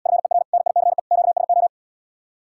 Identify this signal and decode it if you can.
Morse code, heard as a keyed tone, about 77 wpm, 700 Hz. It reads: FER NICE QSO